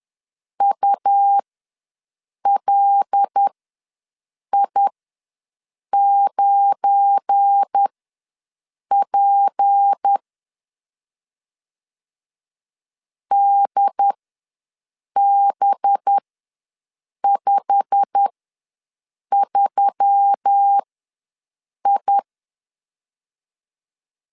Morse code, heard as a keyed tone, about 10 wpm, spaced as for 6 wpm, 800 Hz. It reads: ULI9P DB53I
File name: LEZIONI / 8 YJ45 / 8 YJ45 ws 10wpm 3843